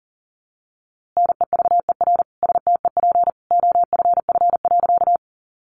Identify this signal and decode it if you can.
Morse code, heard as a keyed tone, about 40 wpm, 700 Hz. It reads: NEVER STEP OFF.
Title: A Sound Of Thunder 40wpm